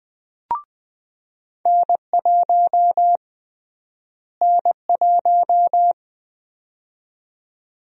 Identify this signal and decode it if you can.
Morse code, heard as a keyed tone, about 20 wpm, 700 Hz. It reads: N1 N1
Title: USA Call Sign Prefixes - Repeated 2x - 20wpm